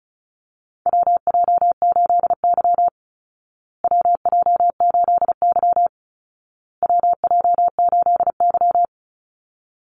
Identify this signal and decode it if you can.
Morse code, heard as a keyed tone, about 34 wpm, 700 Hz. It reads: WJ8Y WJ8Y WJ8Y
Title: Call Signs Repeated 2x - 35wpm